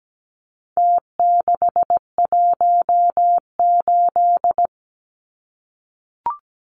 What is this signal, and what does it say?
Morse code, heard as a keyed tone, about 17 wpm, 700 Hz. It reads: T618
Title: Numbers - Contest Style - 4-Digit 17wpm